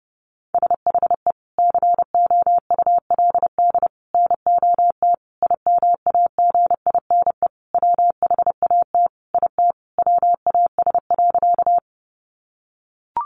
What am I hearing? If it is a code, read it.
Morse code, heard as a keyed tone, about 30 wpm, 700 Hz. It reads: SHE COULD NOT IMAGINE WHAT IT WAS.
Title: Sentences from Top 1400 Words 30wpm